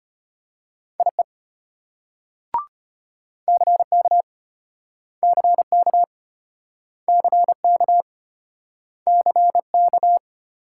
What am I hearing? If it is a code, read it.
Morse code, heard as a keyed tone, about 33 wpm, 700 Hz. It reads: IE CK CK CK CK